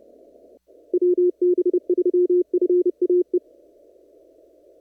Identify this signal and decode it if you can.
Morse code, heard as a keyed tone, about 29 wpm, 360 Hz. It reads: WB3FAE
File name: CW / kt5j wpx cw-op-k5pi